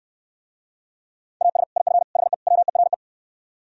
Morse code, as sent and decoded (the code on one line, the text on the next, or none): .- ... / .. ..-. / .... / .-- . .-. .
AS IF HE WERE